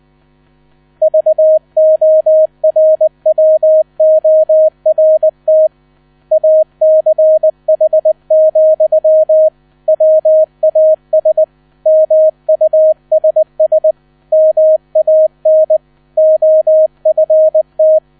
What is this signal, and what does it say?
Morse code, heard as a keyed tone, about 20 wpm, 630 Hz. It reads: VORWORT ACH, WAS MUSS MAN OFT
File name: 000 Vorwort 20wpm